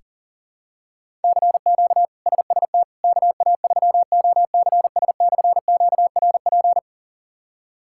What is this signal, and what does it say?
Morse code, heard as a keyed tone, about 40 wpm, 700 Hz. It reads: CQ SST KA3OCS/QRP
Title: K1USN SST QSO Practice - 40wpm